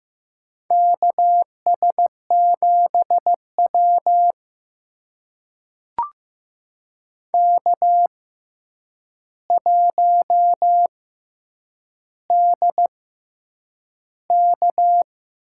Morse code, heard as a keyed tone, about 15 wpm, spaced as for 8 wpm, 700 Hz. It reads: KS7W K1DK
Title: USA Call Signs - 4 Characters - 6x Character Spacing - 15wpm